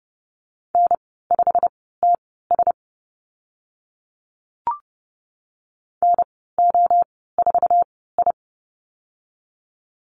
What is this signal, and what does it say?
Morse code, heard as a keyed tone, about 30 wpm, spaced as for 17 wpm, 700 Hz. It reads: N5TS NO4I
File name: Lesson 15g - Number 5 - Call Signs 30wpm